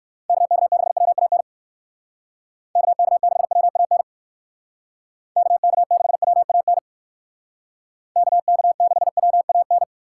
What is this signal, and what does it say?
Morse code, heard as a keyed tone, about 59 wpm, 700 Hz. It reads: KK6WAN KK6WAN KK6WAN KK6WAN